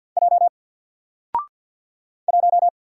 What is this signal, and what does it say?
Morse code, heard as a keyed tone, about 53 wpm, 700 Hz. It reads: J 1
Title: Single Letter-Number 50wpm